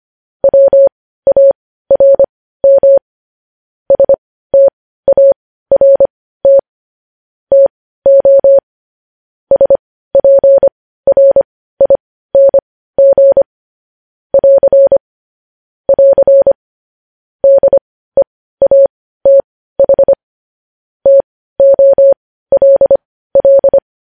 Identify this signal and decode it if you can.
Morse code, heard as a keyed tone, about 25 wpm, spaced as for 15 wpm, 550 Hz. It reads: WARM START TO SPRING <AR> <AR> DEATH TOLL
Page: news headlines in morse code at 15 wpm